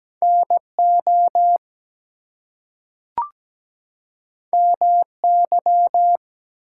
Morse code, heard as a keyed tone, about 17 wpm, 700 Hz. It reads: NO MY
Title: Two Letter Words - 17wpm